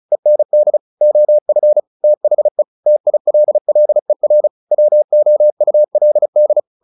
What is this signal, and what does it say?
Morse code, heard as a keyed tone, about 34 wpm, 600 Hz. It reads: END OF THE TILLER WOULD
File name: the old man and the sea 2 35wpm